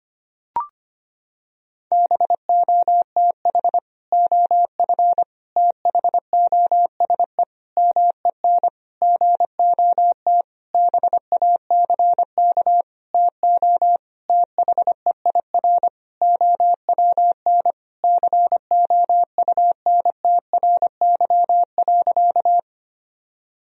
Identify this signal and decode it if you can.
Morse code, heard as a keyed tone, about 25 wpm, 700 Hz. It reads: BOTH OF THOSE MEN GOT BACK TO THEIR OWN COUNTRY.
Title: Sentences from Top 1000 Words - Review Part 1 - 25wpm